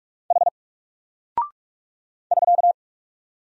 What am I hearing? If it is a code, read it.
Morse code, heard as a keyed tone, about 44 wpm, 700 Hz. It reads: H <SK>
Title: Single Character (Letter:Number:Punctuation:Prosign) Alt 45wpm